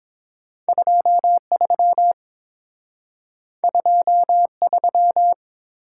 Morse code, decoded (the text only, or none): 23 23